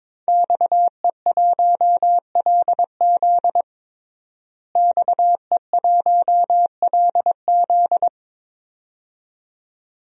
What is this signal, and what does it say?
Morse code, heard as a keyed tone, about 22 wpm, 700 Hz. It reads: XE1LZ XE1LZ
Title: DXCC-100 Call Signs - 5 Characters - Repeated 2x - 22wpm